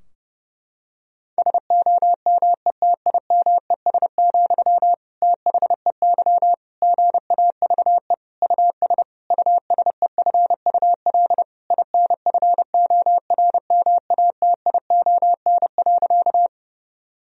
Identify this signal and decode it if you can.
Morse code, heard as a keyed tone, about 30 wpm, 700 Hz. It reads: SOMETIMES, THEY GAVE US USEFUL INFORMATION.
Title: Sentences from Top 1000 Words - Review Part 4 - 30wpm